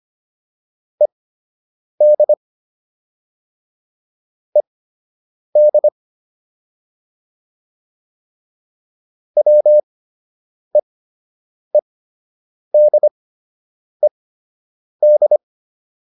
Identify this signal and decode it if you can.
Morse code, heard as a keyed tone, about 25 wpm, spaced as for 8 wpm, 600 Hz. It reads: ED ED WEEDED
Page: fundamental-practice-files